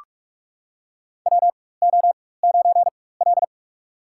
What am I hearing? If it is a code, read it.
Morse code, heard as a keyed tone, about 46 wpm, spaced as for 21 wpm, 700 Hz. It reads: WO9L
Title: USA Call Signs - 4 Characters - 4x Character Spacing - 45wpm